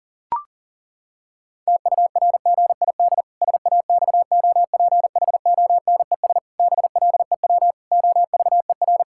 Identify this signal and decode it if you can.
Morse code, heard as a keyed tone, about 40 wpm, 700 Hz. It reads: TURGID SAXOPHONES BLEW OVER
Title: Pangram Challenge v2 - 40wpm